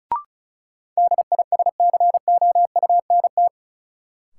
Morse code, heard as a keyed tone, about 34 wpm, 700 Hz. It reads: DISCOUNT
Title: Sentences from Top 1200 Words - New Word Drill - 35wpm